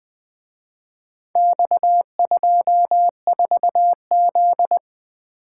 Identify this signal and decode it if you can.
Morse code, heard as a keyed tone, about 24 wpm, 700 Hz. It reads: X24Z
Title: Speed-Racing - Four Letter-Number Combinations - 20wpm